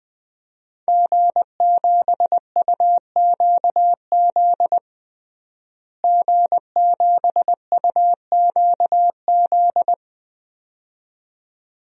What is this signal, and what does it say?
Morse code, heard as a keyed tone, about 20 wpm, 700 Hz. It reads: G7UQZ G7UQZ
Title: DXCC-100 Call Signs - 5 Characters - Repeated 2x - 20wpm